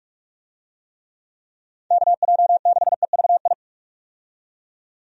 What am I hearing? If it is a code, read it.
Morse code, heard as a keyed tone, about 41 wpm, 700 Hz. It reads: KJ6EVI
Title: USA Call Signs - 6 Characters - 45wpm